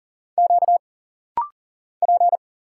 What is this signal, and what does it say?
Morse code, heard as a keyed tone, about 40 wpm, 700 Hz. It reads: Q P
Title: Single Letters - Rapid-Fire 40wpm